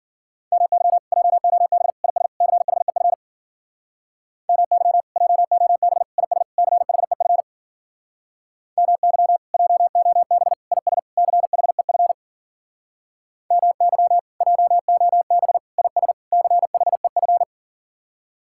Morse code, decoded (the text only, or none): MY JOB IS CHEF MY JOB IS CHEF MY JOB IS CHEF MY JOB IS CHEF